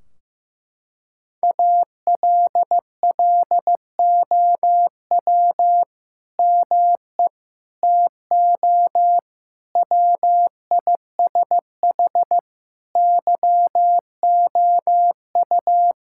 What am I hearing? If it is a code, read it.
Morse code, heard as a keyed tone, about 15 wpm, 700 Hz. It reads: ALLOW ME TO WISH YOU
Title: Sentences from Top 1600 Words 15wpm